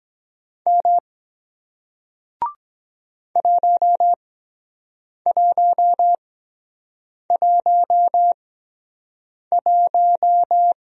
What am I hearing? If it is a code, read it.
Morse code, heard as a keyed tone, about 24 wpm, spaced as for 7 wpm, 700 Hz. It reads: M 1111